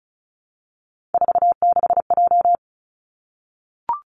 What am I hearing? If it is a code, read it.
Morse code, heard as a keyed tone, about 36 wpm, 700 Hz. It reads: 46J
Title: Three Challenging Letter-Number Combinations 35wpm